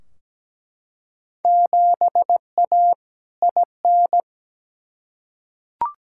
Morse code, heard as a keyed tone, about 17 wpm, 700 Hz. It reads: 7A IN
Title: Field Day Exchanges Encore - 17wpm